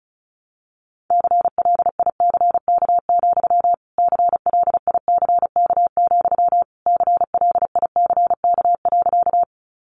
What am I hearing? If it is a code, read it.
Morse code, heard as a keyed tone, about 34 wpm, 700 Hz. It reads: CLICK, CLICK, CLICK.
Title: Sentences from Top 1100 Words 35wpm